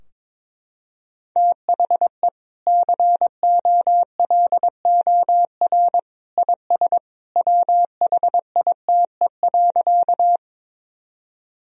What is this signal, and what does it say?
Morse code, heard as a keyed tone, about 22 wpm, 700 Hz. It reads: THE COLOR IS WHITE.
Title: Sentences from Top 1000 Words - Review Part 2 - 22wpm